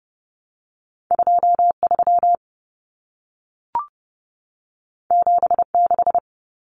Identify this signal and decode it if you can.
Morse code, heard as a keyed tone, about 30 wpm, 700 Hz. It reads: 23 76
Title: Numbers - 2-Digit 30wpm